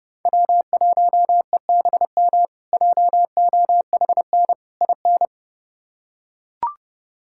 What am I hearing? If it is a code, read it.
Morse code, heard as a keyed tone, about 30 wpm, 700 Hz. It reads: W1EBM JOHN IN
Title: K1USN SST QSO Practice - 30wpm